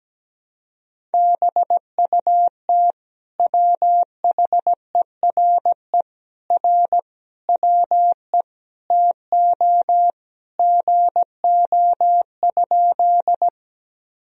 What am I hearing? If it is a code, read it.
Morse code, heard as a keyed tone, about 17 wpm, 700 Hz. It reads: BUT WHERE R WE TO GO?
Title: Sentences from Top 200 Words and Abbreviations 17wpm